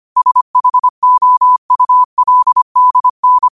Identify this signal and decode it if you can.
Morse code, heard as a keyed tone, about 25 wpm, 1000 Hz. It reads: SHOULDN
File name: qotdfiles / 25wpm